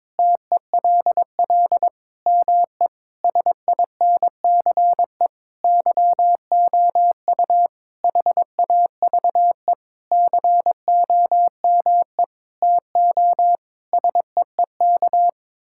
Morse code, heard as a keyed tone, about 22 wpm, 700 Hz. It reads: TELL ME SINCE YOU HAVE COME TO SEEK
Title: Sentences from Top 1600 Words 22wpm